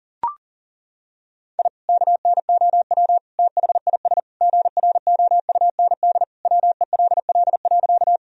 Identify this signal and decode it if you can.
Morse code, heard as a keyed tone, about 37 wpm, 700 Hz. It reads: I KNOW THIS GROUND WELL.
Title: Sentences from Top 1000 Words - Review Part 2 - 40wpm